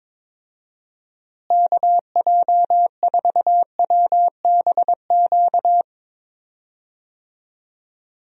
Morse code, heard as a keyed tone, about 22 wpm, 700 Hz. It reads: KJ4WBQ
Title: USA Call Signs - 6 Characters - Repeated 2x - 22wpm